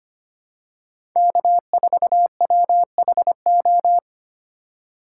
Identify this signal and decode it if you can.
Morse code, heard as a keyed tone, about 25 wpm, 700 Hz. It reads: K4WHO
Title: USA Call Signs - 5 Characters - 2x Character Spacing - 25wpm